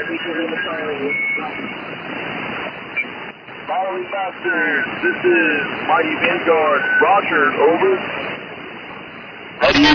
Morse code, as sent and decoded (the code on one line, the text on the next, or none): none